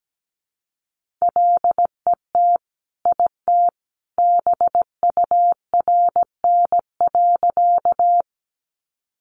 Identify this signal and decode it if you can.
Morse code, heard as a keyed tone, about 17 wpm, 700 Hz. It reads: LET IT BURN.
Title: Sentences from Top 1300 Words 17wpm